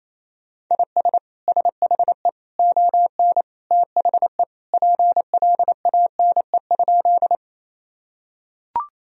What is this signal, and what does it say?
Morse code, heard as a keyed tone, about 28 wpm, 700 Hz. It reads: IS SHE ON THE PLANE?
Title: Sentences from Top 1000 Words - Review Part 3 - 28wpm